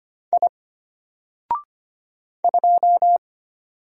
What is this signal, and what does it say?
Morse code, heard as a keyed tone, about 25 wpm, 700 Hz. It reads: I 2
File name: Single-Letter-Number-25wpm